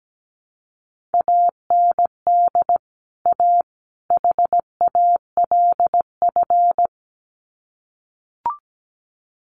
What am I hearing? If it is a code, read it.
Morse code, heard as a keyed tone, about 17 wpm, 700 Hz. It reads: AND A HALF
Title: Sets of 3 Words 17wpm